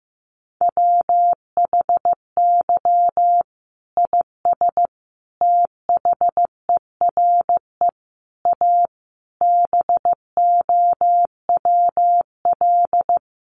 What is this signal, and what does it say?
Morse code, heard as a keyed tone, about 15 wpm, 700 Hz. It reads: WHY IS THERE A BOWL
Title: Sentences from Top 1200 Words 15wpm